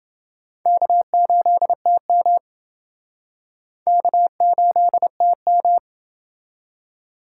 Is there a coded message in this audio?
Yes: Morse code from 0.7 to 5.8 s, heard as a keyed tone, about 28 wpm, 700 Hz. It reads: K8TM K8TM